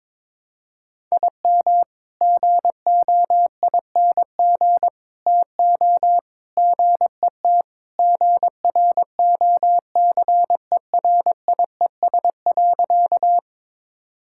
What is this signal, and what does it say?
Morse code, heard as a keyed tone, about 22 wpm, 700 Hz. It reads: IM GOING TO GET GROCERIES.
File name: practice / 22wpm / Sentences with 3-Word Phrases - 22wpm